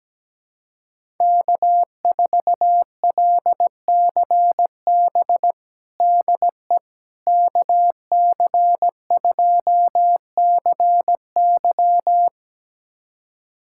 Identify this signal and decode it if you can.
Morse code, heard as a keyed tone, about 17 wpm, 700 Hz. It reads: K4LCB DE KC2CY
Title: USA Callsign DE Callsign - 5 Characters - 17wpm